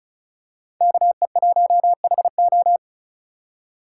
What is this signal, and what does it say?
Morse code, heard as a keyed tone, about 36 wpm, 700 Hz. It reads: KE1HO
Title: USA Call Signs - 5 Characters - 2x Character Spacing - 35wpm